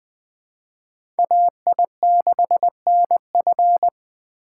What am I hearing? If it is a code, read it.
Morse code, heard as a keyed tone, about 23 wpm, 700 Hz. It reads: AI6NF
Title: Speed Racing - USA 2x2 Call Signs 20wpm